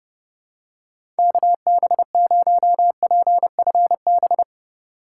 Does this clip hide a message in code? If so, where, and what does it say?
Yes: Morse code from 1.2 to 4.4 s, heard as a keyed tone, about 28 wpm, 700 Hz. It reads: KB0PFB